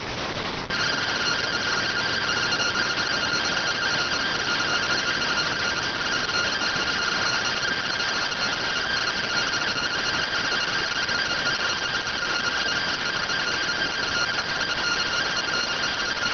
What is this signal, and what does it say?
Two-tone teleprinter signal, about 170 Hz shift.